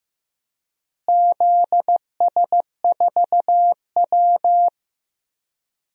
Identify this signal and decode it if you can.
Morse code, heard as a keyed tone, about 15 wpm, 700 Hz. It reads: ZS4W
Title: DXCC-100 Call Signs - 4 Characters - Repeated 3x - 15wpm